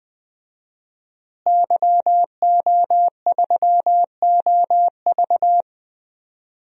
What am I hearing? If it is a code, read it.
Morse code, heard as a keyed tone, about 20 wpm, 700 Hz. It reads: YO3OV
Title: DXCC-100 Call Signs - 5 Characters - Repeated 3x - 20wpm